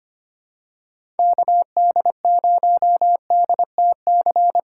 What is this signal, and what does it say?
Morse code, heard as a keyed tone, about 25 wpm, 700 Hz. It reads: KD0DTC
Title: USA Call Signs - 6 Characters - 3x Character Spacing - 25wpm